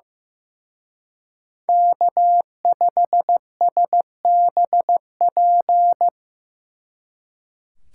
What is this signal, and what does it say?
Morse code, heard as a keyed tone, about 18 wpm, 700 Hz. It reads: K5SBP
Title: Speed Racing - USA Call Signs 15wpm